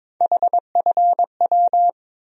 Morse code, heard as a keyed tone, about 22 wpm, 700 Hz. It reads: HFW
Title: USA Call Sign Postfixes - 3 Letters 22wpm